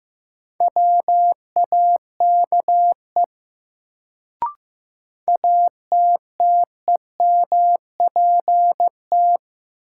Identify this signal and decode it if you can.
Morse code, heard as a keyed tone, about 15 wpm, 700 Hz. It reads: WAKE ATTEMPT
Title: Sets of 4 Words - New Word Drill - 15wpm